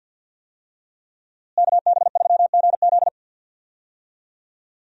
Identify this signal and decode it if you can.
Morse code, heard as a keyed tone, about 51 wpm, 700 Hz. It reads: KB3GZ
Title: USA Call Signs - 5 Characters - Repeated 2x - 50wpm